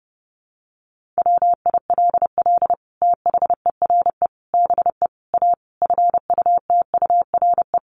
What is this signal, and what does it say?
Morse code, heard as a keyed tone, about 30 wpm, 700 Hz. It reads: WILL THERE BE A FUTURE
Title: Sentences from Top 1400 Words 30wpm